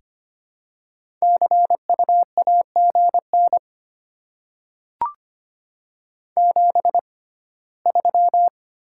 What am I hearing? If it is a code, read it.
Morse code, heard as a keyed tone, about 25 wpm, spaced as for 17 wpm, 700 Hz. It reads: CU AGN 73
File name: Top 20 QSO Elements - 6x Character Spacing - 25wpm